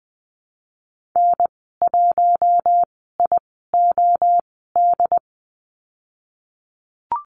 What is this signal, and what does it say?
Morse code, heard as a keyed tone, about 20 wpm, spaced as for 15 wpm, 700 Hz. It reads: N1IOD
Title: 069 Lesson 12g - Letter d - Call Signs 20wpm